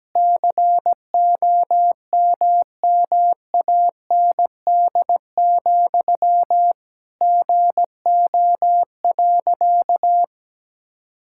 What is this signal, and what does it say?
Morse code, heard as a keyed tone, about 17 wpm, 700 Hz. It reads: COMMAND, GO.
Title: Sentences from Top 1600 Words 17wpm